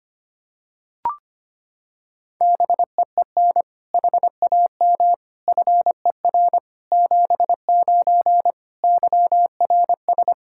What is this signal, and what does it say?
Morse code, heard as a keyed tone, about 25 wpm, 700 Hz. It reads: BEEN HAM FER 79 YRS
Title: Practice QSO - Exchange 3 - More Info - 25wpm